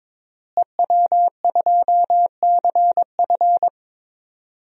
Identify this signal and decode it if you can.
Morse code, heard as a keyed tone, about 22 wpm, 700 Hz. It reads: EW2CF
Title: DXCC-100 Call Signs - 5 Characters - Repeated 3x - 22wpm